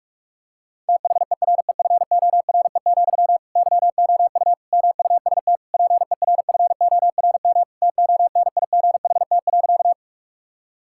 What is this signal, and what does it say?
Morse code, heard as a keyed tone, about 46 wpm, 700 Hz. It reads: THEREFORE, YOU MUST PERFORM TONIGHT.
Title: Sentences from Top 1000 Words - Review Part 5 - 45wpm